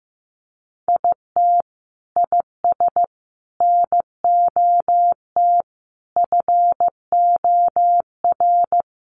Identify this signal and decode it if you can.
Morse code, heard as a keyed tone, about 15 wpm, 700 Hz. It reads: IT IS NOT FOR
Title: Sentences from Top 1300 Words 15wpm